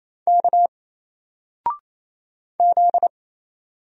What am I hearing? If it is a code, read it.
Morse code, heard as a keyed tone, about 27 wpm, 700 Hz. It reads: K Z
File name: Single-Letter-Number-28wpm